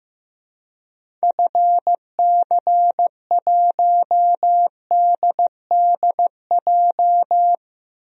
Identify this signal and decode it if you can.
Morse code, heard as a keyed tone, about 15 wpm, 700 Hz. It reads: FC1DDJ
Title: DXCC-100 Call Signs - 6 Characters - 15wpm